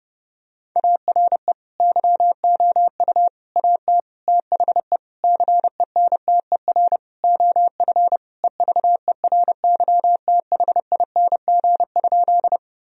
Morse code, heard as a keyed tone, about 32 wpm, 700 Hz. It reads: ARE YOU AT THE CENTER OF EVERYTHING?
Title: Sentences from Top 1000 Words - Review Part 2 - 30wpm